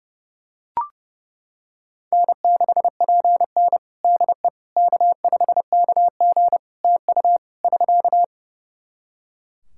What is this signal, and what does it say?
Morse code, heard as a keyed tone, about 32 wpm, 700 Hz. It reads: N6PN DE K5KG TU <SK>
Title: Practice QSO - Exchange 4 - Ending - 30wpm